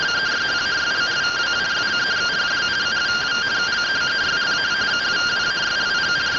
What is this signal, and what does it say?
Two-tone teleprinter signal, about 170 Hz shift.
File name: findings / CT 1 Scout 0002.gen-25 rtty rx